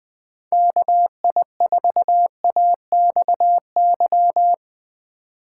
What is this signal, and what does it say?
Morse code, heard as a keyed tone, about 20 wpm, 700 Hz. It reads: KI4AXY
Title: USA Call Signs - 3x Character Spacing - 20wpm